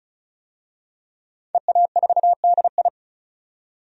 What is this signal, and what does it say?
Morse code, heard as a keyed tone, about 36 wpm, 700 Hz. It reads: EA4DI
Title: DXCC-100 Call Signs - 5 Characters - 35wpm